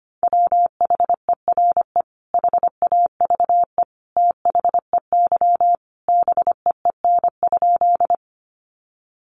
Morse code, heard as a keyed tone, about 25 wpm, 700 Hz. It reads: WHERE HAVE THEY BEEN?
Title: Sentences from Top 200 Words 25wpm